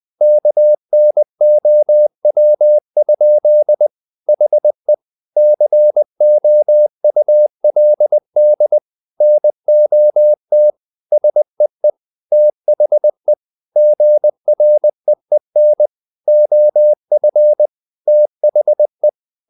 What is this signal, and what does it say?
Morse code, heard as a keyed tone, about 20 wpm, 600 Hz. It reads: KNOW? HE COULD NOT SEE THE GREEN OF THE
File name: the old man and the sea / the old man and the sea part 1 20wpm